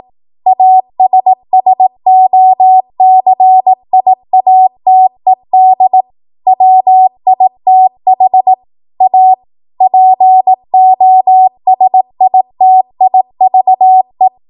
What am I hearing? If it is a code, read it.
Morse code, heard as a keyed tone, about 18 wpm, 750 Hz. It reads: ASSOCIATED WITH A POSITIVE
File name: Morse / 18WPM